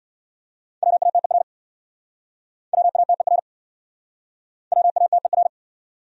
Morse code, heard as a keyed tone, about 67 wpm, 690 Hz. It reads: WATER WATER WATER